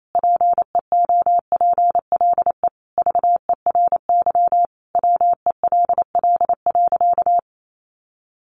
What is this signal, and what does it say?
Morse code, heard as a keyed tone, about 28 wpm, 700 Hz. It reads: PEOPLE VERY WELL.
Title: Sentences from Top 1300 Words 28wpm